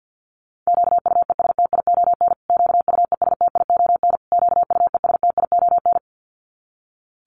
Morse code, heard as a keyed tone, about 49 wpm, 700 Hz. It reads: QUESTION QUESTION QUESTION
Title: Top 500 Words Repeated 50wpm